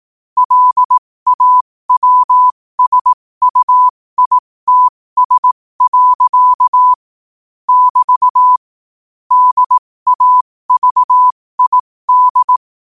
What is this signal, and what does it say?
Morse code, heard as a keyed tone, about 18 wpm, spaced as for 15 wpm, 1000 Hz. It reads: LAWSUITS. <BT> DAVID
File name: qotdfiles / 15wpm